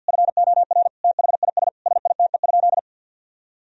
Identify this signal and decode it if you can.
Morse code, heard as a keyed tone, about 50 wpm, 700 Hz. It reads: FOR THIS SITE?
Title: Sentences from Top 1000 Words - Review Part 4 - 50wpm